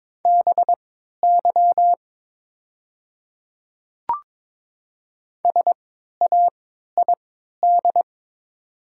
Morse code, heard as a keyed tone, about 22 wpm, spaced as for 12 wpm, 700 Hz. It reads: BY SAID
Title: 3x Character Spacing - Top 100 Words - 22wpm